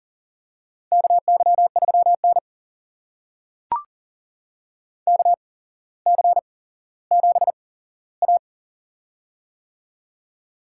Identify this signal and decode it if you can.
Morse code, heard as a keyed tone, about 40 wpm, spaced as for 17 wpm, 700 Hz. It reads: KY3N KC7A
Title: USA Call Signs - 4 Characters - 8x Character Spacing - 40wpm